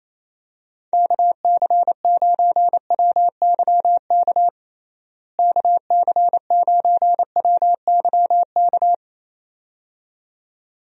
Morse code, heard as a keyed tone, about 28 wpm, 700 Hz. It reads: KC9WYK KC9WYK
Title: USA Call Signs - 6 Characters - Repeated 2x - 28wpm